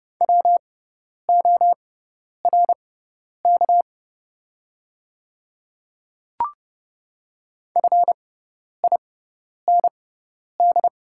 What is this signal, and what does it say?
Morse code, heard as a keyed tone, about 30 wpm, spaced as for 10 wpm, 700 Hz. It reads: WORK FIND
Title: 6x Character Spacing - Top 200 Words - 30wpm